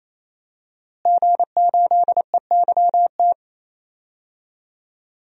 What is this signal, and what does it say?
Morse code, heard as a keyed tone, about 28 wpm, 700 Hz. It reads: G8EYT
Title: DXCC-100 Call Signs - 5 Characters - Repeated 3x - 28wpm